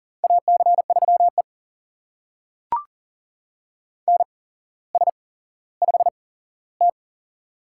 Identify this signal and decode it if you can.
Morse code, heard as a keyed tone, about 40 wpm, spaced as for 17 wpm, 700 Hz. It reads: AC3E NS5T
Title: USA Call Signs - 4 Characters - 8x Character Spacing - 40wpm